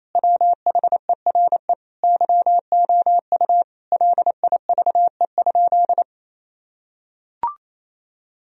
Morse code, heard as a keyed tone, about 28 wpm, 700 Hz. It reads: WHERE YOU LIVE?
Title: Sentences from Top 1000 Words - Review Part 4 - 28wpm